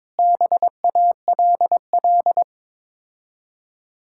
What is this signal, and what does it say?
Morse code, heard as a keyed tone, about 22 wpm, 700 Hz. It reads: BALL
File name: practice / 22wpm / Difficult Words with Hints - 22wpm